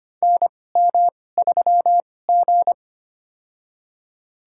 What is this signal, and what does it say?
Morse code, heard as a keyed tone, about 25 wpm, spaced as for 18 wpm, 700 Hz. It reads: NM3G
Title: USA Call Signs - 4 Characters - 2x Character Spacing - 25wpm